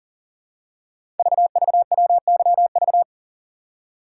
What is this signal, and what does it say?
Morse code, heard as a keyed tone, about 42 wpm, 700 Hz. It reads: VVWYV
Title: Code Groups - 5 Letters - 40wpm